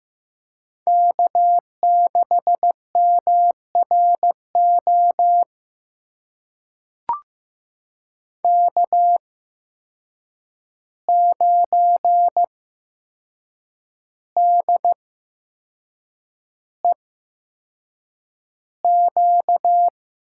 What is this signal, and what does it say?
Morse code, heard as a keyed tone, about 15 wpm, spaced as for 6 wpm, 700 Hz. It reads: K6MRO K9DEQ
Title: USA Call Signs - 8x Character Spacing - 15wpm